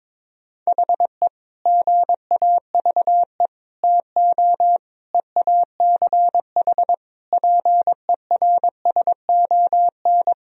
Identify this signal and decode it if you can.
Morse code, heard as a keyed tone, about 22 wpm, 700 Hz. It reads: HE GAVE TO EACH PERSON
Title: Sentences from Top 1000 Words - Review Part 4 - 22wpm